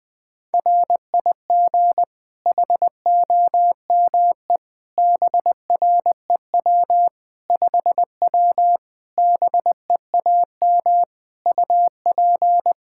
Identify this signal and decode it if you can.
Morse code, heard as a keyed tone, about 21 wpm, 700 Hz. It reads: RIG HOME BREW 5W BEAM UP
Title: K1USN SST QSO Practice 20wpm